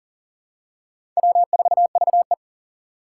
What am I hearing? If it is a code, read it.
Morse code, heard as a keyed tone, about 42 wpm, 700 Hz. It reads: W4VE
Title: Four Challenging Letter-Number Combinations 40wpm